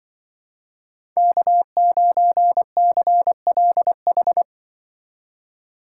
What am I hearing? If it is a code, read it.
Morse code, heard as a keyed tone, about 24 wpm, 700 Hz. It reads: K9CLH
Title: Speed Racing - USA Call Signs 20wpm